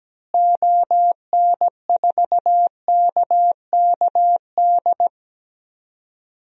Morse code, heard as a keyed tone, about 17 wpm, 700 Hz. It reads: ON4KKD
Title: DXCC-100 Call Signs - 6 Characters - Repeated 2x - 17wpm